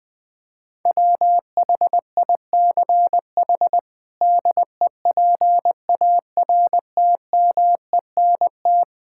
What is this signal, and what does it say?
Morse code, heard as a keyed tone, about 21 wpm, 700 Hz. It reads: WHICH DEPARTMENT
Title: Sentences from Top 1000 Words - Review Part 5 - 20wpm